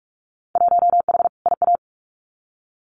Morse code, heard as a keyed tone, about 46 wpm, 700 Hz. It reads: 1H IA
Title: Winter Field Day Exchanges 45wpm